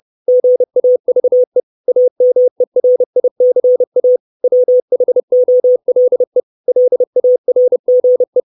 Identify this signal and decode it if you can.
Morse code, heard as a keyed tone, about 30 wpm, 500 Hz. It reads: GAVE AMERICA WHOLE LARGE
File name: 632 Words 30WPM